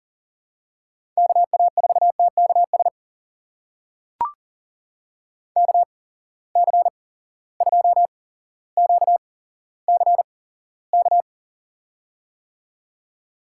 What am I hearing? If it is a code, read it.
Morse code, heard as a keyed tone, about 40 wpm, spaced as for 17 wpm, 700 Hz. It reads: KA4TKS KC2QCK
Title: USA Call Signs - 6 Characters - 8x Character Spacing - 40wpm